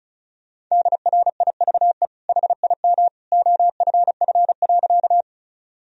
Morse code, heard as a keyed tone, about 36 wpm, 700 Hz. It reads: DRIVE HIM OFF.
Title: Sentences from Top 1000 Words - Review Part 2 - 35wpm